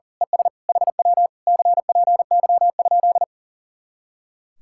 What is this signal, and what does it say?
Morse code, heard as a keyed tone, about 38 wpm, 700 Hz. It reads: ES HW CPY?
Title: Speed-Racing Common QSO Phrases - 40wpm